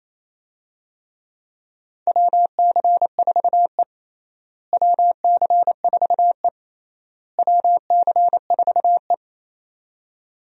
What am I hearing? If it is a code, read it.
Morse code, heard as a keyed tone, about 27 wpm, 700 Hz. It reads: WC4E WC4E WC4E
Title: USA Call Signs - 4 Characters - Repeated 3x - 28wpm